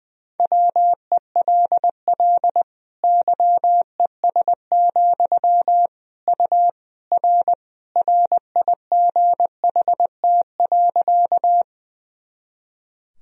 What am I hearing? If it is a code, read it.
Morse code, heard as a keyed tone, about 21 wpm, 700 Hz. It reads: WELL YES, U R RIGHT.
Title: Sentences from Top 500 Words and Abbreviations 20wpm